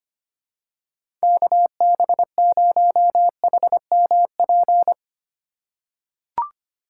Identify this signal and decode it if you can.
Morse code, heard as a keyed tone, about 25 wpm, 700 Hz. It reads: KB0HMP
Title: USA Call Signs - 6x Character Spacing - 25wpm